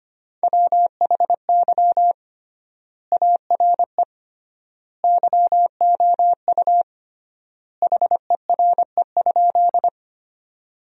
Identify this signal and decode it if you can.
Morse code, heard as a keyed tone, about 25 wpm, 700 Hz. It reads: WHY ARE YOU HERE?
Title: 3x Word-Spacing - Sentences from Top 200 Words - 25wpm